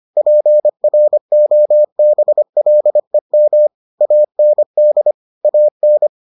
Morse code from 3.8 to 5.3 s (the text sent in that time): AND